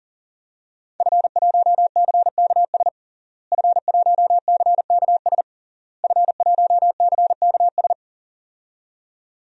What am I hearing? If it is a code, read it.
Morse code, heard as a keyed tone, about 40 wpm, 700 Hz. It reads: F1CKS F1CKS F1CKS
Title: DXCC-100 Call Signs - 5 Characters - Repeated 3x - 40wpm